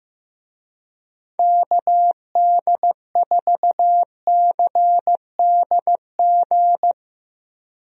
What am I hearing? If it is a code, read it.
Morse code, heard as a keyed tone, about 15 wpm, 700 Hz. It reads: KD4CDG
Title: USA Call Signs - 6 Characters - Repeated 2x - 15wpm